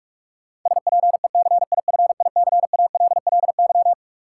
Morse code, heard as a keyed tone, about 46 wpm, 700 Hz. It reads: SPECIFICALLY
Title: Sentences from Top 1100 Words - New Word Drill - 45wpm